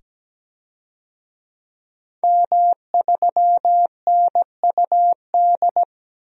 Morse code, heard as a keyed tone, about 17 wpm, 700 Hz. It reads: M3NUD
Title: DXCC-100 Call Signs - 5 Characters - 17wpm